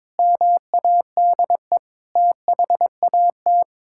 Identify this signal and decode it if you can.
Morse code, heard as a keyed tone, about 22 wpm, 700 Hz. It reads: MADE THAT
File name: practice / 22wpm / Sentences with 8-Word Phrases - 22wpm